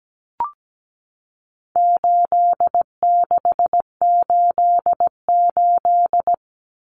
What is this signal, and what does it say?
Morse code, heard as a keyed tone, about 17 wpm, 700 Hz. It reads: 8688
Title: Numbers - 4-Digit 17wpm